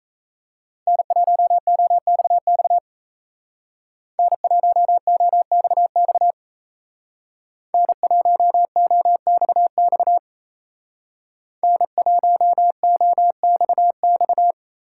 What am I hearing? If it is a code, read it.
Morse code, heard as a keyed tone, about 33 wpm, 700 Hz. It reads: N1OXX N1OXX N1OXX N1OXX
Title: Speed Racing - USA 1x3 Call Signs 28wpm